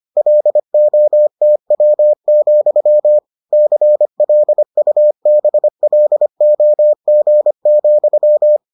Morse code, heard as a keyed tone, about 25 wpm, 600 Hz. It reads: LOTW, CLUBLOG,